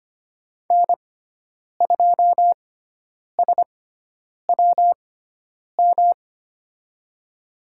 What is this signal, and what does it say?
Morse code, heard as a keyed tone, about 25 wpm, spaced as for 9 wpm, 700 Hz. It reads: N2SWM
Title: USA Call Signs - 5 Characters - 6x Character Spacing - 25wpm